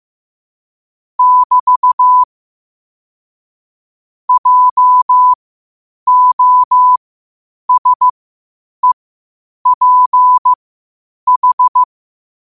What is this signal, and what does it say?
Morse code, heard as a keyed tone, about 15 wpm, spaced as for 8 wpm, 1000 Hz. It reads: <BT> JOSEPH
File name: qotdfiles / 8wpm